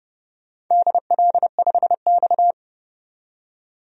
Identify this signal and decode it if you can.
Morse code, heard as a keyed tone, about 31 wpm, 700 Hz. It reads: DL5X
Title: DXCC-100 Call Signs - 4 Characters - 30wpm